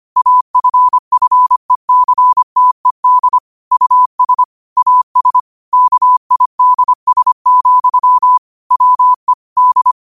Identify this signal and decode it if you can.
Morse code, heard as a keyed tone, about 25 wpm, 1000 Hz. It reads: AFFECTED US AS KIDS, WE D
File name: qotdfiles / 25wpm